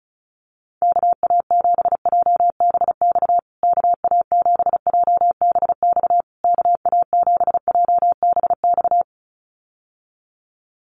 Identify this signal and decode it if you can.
Morse code, heard as a keyed tone, about 34 wpm, 700 Hz. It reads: KA7JBX KA7JBX KA7JBX
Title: USA Call Signs Repeated 3x 35wpm